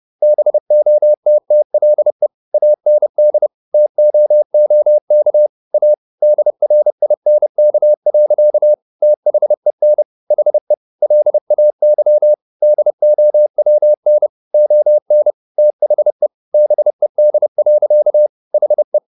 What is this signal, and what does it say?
Morse code, heard as a keyed tone, about 30 wpm, 600 Hz. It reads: BOTTLE AND TOOK A DRINK. THEN HE LAY DOWN ON THE BED. HE